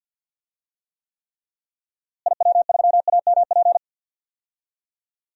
Morse code, heard as a keyed tone, about 52 wpm, 700 Hz. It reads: IW3AMP
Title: DXCC-100 Call Signs - 6 Characters - Repeated 3x - 50wpm